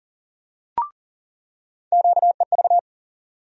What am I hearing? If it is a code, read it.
Morse code, heard as a keyed tone, about 40 wpm, 700 Hz. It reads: QEV
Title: USA Call Sign Postfixes - 3 Letters 40wpm